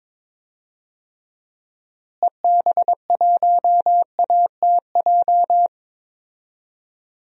Morse code, heard as a keyed tone, about 22 wpm, 700 Hz. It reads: EB1ATJ
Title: DXCC-100 Call Signs - 6 Characters - Repeated 2x - 22wpm